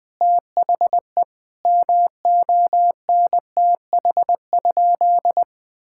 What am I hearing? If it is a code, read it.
Morse code, heard as a keyed tone, about 21 wpm, 700 Hz. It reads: THE MONTH?
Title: Sentences from Top 1000 Words - Review Part 6 - 20wpm